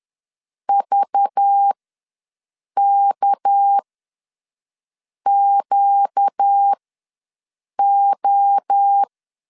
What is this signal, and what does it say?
Morse code, heard as a keyed tone, about 10 wpm, spaced as for 5 wpm, 800 Hz. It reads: VKQO
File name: LEZIONI / 6 PKX0 / 6 PK ws 10wpm 3745